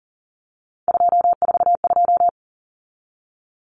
Morse code, heard as a keyed tone, about 40 wpm, 700 Hz. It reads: 242
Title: Numbers - 3-Digit 40wpm